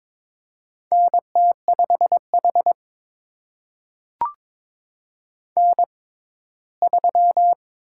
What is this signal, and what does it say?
Morse code, heard as a keyed tone, about 22 wpm, spaced as for 15 wpm, 700 Hz. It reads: NT5H N3
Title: USA Call Signs - 4 Characters - 6x Character Spacing - 22wpm